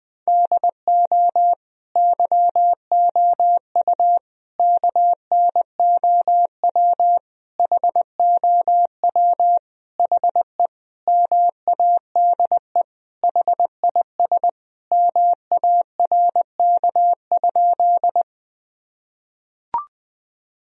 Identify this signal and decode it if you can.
Morse code, heard as a keyed tone, about 20 wpm, 700 Hz. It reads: DO YOU KNOW HOW HE MADE HIS MARK?
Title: Sentences from Top 1000 Words - Review Part 1 - 20wpm